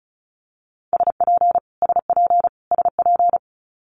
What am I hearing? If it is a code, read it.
Morse code, heard as a keyed tone, about 34 wpm, 700 Hz. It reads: SP SP SP
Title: Multi-letter Phonemes Repeated 3x 35wpm